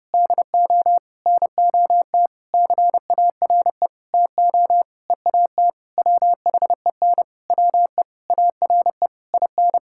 Morse code, heard as a keyed tone, about 32 wpm, 700 Hz. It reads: DO NOT CARE TO EAT WHEN WE ARE IN
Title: Sentences from Top 1600 Words 30wpm